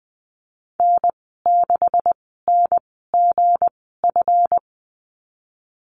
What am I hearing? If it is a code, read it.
Morse code, heard as a keyed tone, about 20 wpm, spaced as for 14 wpm, 700 Hz. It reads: N6NGF
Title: 231 Lesson 37g - Letter x - Call Signs 20wpm